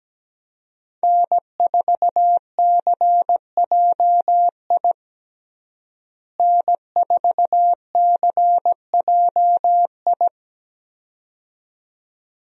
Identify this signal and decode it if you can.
Morse code, heard as a keyed tone, about 17 wpm, 700 Hz. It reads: N4CJI N4CJI
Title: USA Call Signs - 5 Characters - Repeated 2x - 17wpm